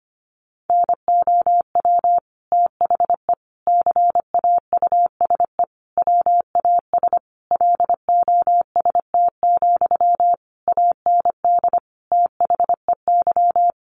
Morse code, heard as a keyed tone, about 25 wpm, 700 Hz. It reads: NOW THE CAUSE WAS LOST, AND THEY
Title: Sentences from Top 1500 Words 25wpm